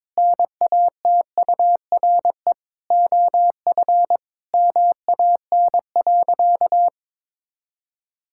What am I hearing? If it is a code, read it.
Morse code, heard as a keyed tone, about 21 wpm, 700 Hz. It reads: NATURE OF MAN.
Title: Sentences from Top 1000 Words - Review Part 6 - 22wpm